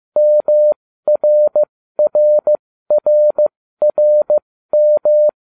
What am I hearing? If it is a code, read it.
Morse code, heard as a keyed tone, about 15 wpm, spaced as for 13 wpm, 600 Hz. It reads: MRRRRM